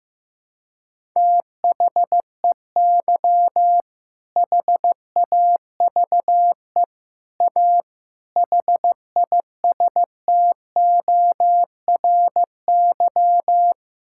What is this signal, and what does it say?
Morse code, heard as a keyed tone, about 15 wpm, 700 Hz. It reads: THEY HAVE A HISTORY
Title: Sentences from Top 1000 Words - Review Part 5 - 15wpm